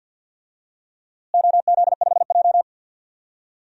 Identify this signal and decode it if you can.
Morse code, heard as a keyed tone, about 53 wpm, 700 Hz. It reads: OZ5J
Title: DXCC-100 Call Signs - 4 Characters - Repeated 2x - 50wpm